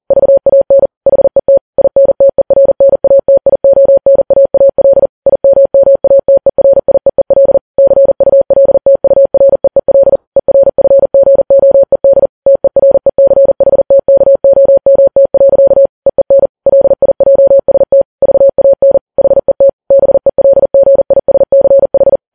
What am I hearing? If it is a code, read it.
Morse code, heard as a keyed tone, about 40 wpm, 550 Hz. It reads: VAN HET INTERNATIONAAL IMMATERIEEL CULTUREEL ERFGOED TERECHTKOMT. EEN LIJST VAN HET BELGISCH
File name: Files Web tls / Morse on9fv / morse40kps